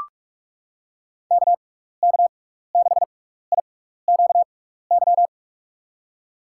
Morse code, heard as a keyed tone, about 44 wpm, spaced as for 15 wpm, 700 Hz. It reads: KK6IQY
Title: USA Call Signs - 6 Characters - 6x Character Spacing - 45wpm